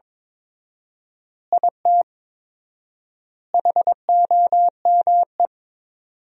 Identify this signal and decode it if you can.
Morse code, heard as a keyed tone, about 22 wpm, 700 Hz. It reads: IT HOME
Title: 4x Word-Spacing - Sentences from Top 500 Words - 22wpm